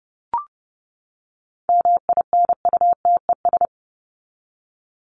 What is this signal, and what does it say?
Morse code, heard as a keyed tone, about 30 wpm, 700 Hz. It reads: MINUTES
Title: ES:S Suffix - Top 100 Words with Base Words 30wpm